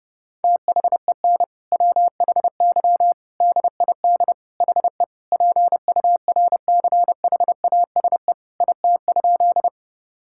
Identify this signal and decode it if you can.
Morse code, heard as a keyed tone, about 32 wpm, 700 Hz. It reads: THEN WHY DID HE PURCHASE IT?
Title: Sentences from Top 1000 Words - Review Part 5 - 30wpm